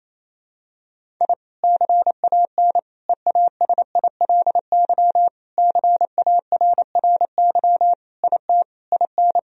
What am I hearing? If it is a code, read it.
Morse code, heard as a keyed tone, about 28 wpm, 700 Hz. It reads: I CAN EASILY CARRY IT IN
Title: Sentences from Top 1000 Words - Review Part 1 - 28wpm